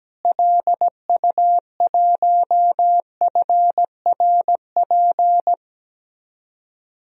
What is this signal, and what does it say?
Morse code, heard as a keyed tone, about 17 wpm, 700 Hz. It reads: LU1FRP
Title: DXCC-100 Call Signs - 6 Characters - Repeated 2x - 17wpm